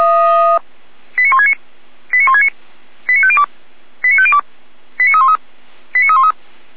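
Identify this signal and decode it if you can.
Five-tone selective call (ZVEI), heard as quick runs of five tones, tones of about 70 ms. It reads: 89169 89169 89502 89502 89313 89313
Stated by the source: ZVEI five-tone selective call received by SDR, hosted by Feuerwehr Orlamünde.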